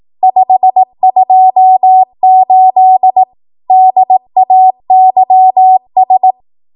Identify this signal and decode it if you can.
Morse code, heard as a keyed tone, about 18 wpm, 750 Hz. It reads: 528 DAYS
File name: Archive / 18 WPM / 18WPM